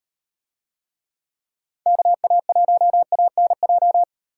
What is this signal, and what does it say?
Morse code, heard as a keyed tone, about 38 wpm, 700 Hz. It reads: KA1ANJ